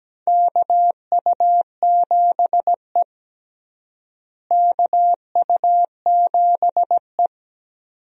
Morse code, heard as a keyed tone, about 17 wpm, 700 Hz. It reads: KU7E KU7E
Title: USA Call Signs - 4 Characters - Repeated 2x - 17wpm